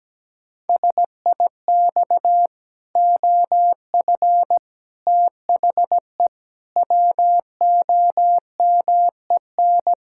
Morse code, heard as a keyed tone, about 17 wpm, 700 Hz. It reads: SIX OF THE WOMEN
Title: Pangram Challenge v2 - 17wpm